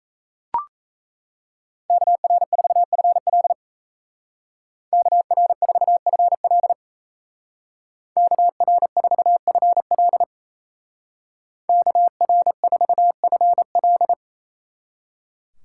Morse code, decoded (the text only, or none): KR4FL KR4FL KR4FL KR4FL